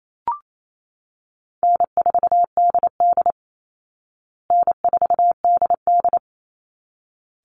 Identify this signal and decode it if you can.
Morse code, heard as a keyed tone, about 28 wpm, 700 Hz. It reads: N4DD N4DD
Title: Easy Call Signs Repeated 2x - 28wpm